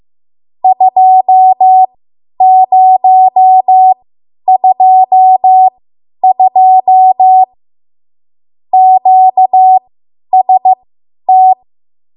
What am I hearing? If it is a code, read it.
Morse code, heard as a keyed tone, about 15 wpm, spaced as for 10 wpm, 750 Hz. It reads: 2022 QST
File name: Morse / 10WPM